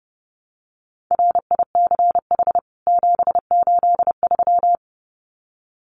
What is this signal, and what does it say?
Morse code, heard as a keyed tone, about 28 wpm, 700 Hz. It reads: RICH 783
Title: CWT Exchange 30wpm